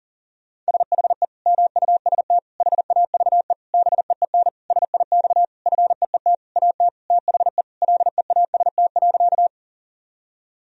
Morse code, heard as a keyed tone, about 40 wpm, 700 Hz. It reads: SHE MUST HAVE BEEN SIX FEET AT THE LEAST.
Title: Sentences from Top 1000 Words - Review Part 2 - 40wpm